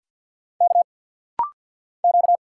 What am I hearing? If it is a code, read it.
Morse code, heard as a keyed tone, about 48 wpm, 700 Hz. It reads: K Q
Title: Single Letter-Number - Rapid-Fire 50wpm